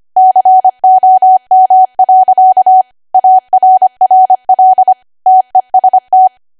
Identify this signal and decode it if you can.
Morse code, heard as a keyed tone, about 25 wpm, 750 Hz. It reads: COM. ARRL TEST
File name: Archive / 25 WPM / 25WPM